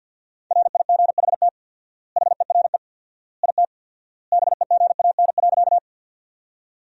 Morse code, heard as a keyed tone, about 48 wpm, 700 Hz. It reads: RIGHT HERE IT BEGAN.
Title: 4x Word-Spacing - Sentences from Top 300 Words - 50wpm